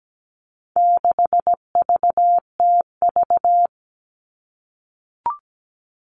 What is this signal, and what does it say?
Morse code, heard as a keyed tone, about 17 wpm, 700 Hz. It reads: 6VTV
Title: All Cut Numbers - 4-Digit 17wpm